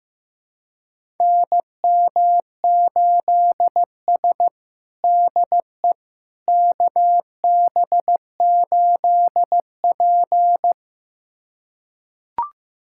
Morse code, heard as a keyed tone, about 15 wpm, 700 Hz. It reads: NM8S DE KB8P
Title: USA Callsign DE Callsign - 4 Characters - 15wpm